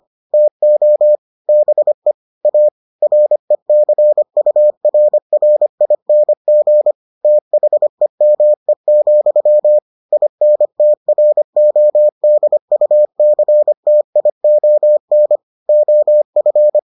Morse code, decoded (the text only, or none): TO BE A RECURRING THEME, INTRODUCTION OF